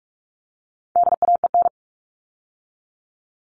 Morse code, heard as a keyed tone, about 44 wpm, 700 Hz. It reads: DAEN
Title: All Cut Numbers - 4-Digit 45wpm